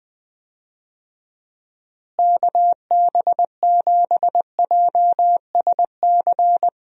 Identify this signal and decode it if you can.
Morse code, heard as a keyed tone, about 20 wpm, 700 Hz. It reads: KB7JSC
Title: USA Call Signs - 6 Characters - Repeated 2x - 20wpm